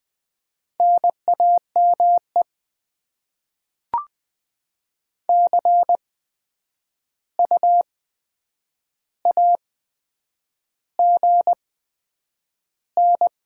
Morse code, heard as a keyed tone, about 20 wpm, spaced as for 9 wpm, 700 Hz. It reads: NAME CU AGN
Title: Top 20 QSO Elements - 8x Character Spacing - 20wpm